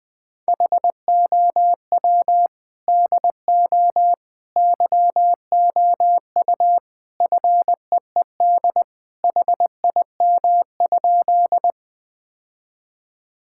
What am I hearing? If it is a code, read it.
Morse code, heard as a keyed tone, about 21 wpm, 700 Hz. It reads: HOW DO YOU FEED HIM?
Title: Sentences from Top 1600 Words 20wpm